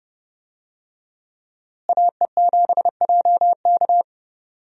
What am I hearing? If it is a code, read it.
Morse code, heard as a keyed tone, about 30 wpm, 700 Hz. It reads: AE7JK
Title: USA Call Signs - 5 Characters - Repeated 2x - 30wpm